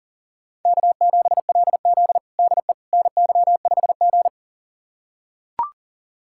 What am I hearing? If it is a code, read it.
Morse code, heard as a keyed tone, about 40 wpm, 700 Hz. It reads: K7LZ DE NY5G
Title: USA Callsign DE Callsign - 4 Characters - 40wpm